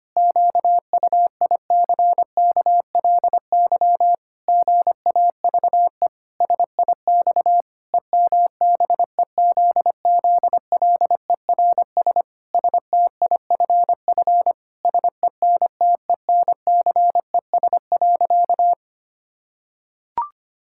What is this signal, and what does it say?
Morse code, heard as a keyed tone, about 25 wpm, 700 Hz. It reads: QUICKLY GAVE SIX EMBEZZLERS STIFF SENTENCES.
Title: Pangram Challenge v2 - 25wpm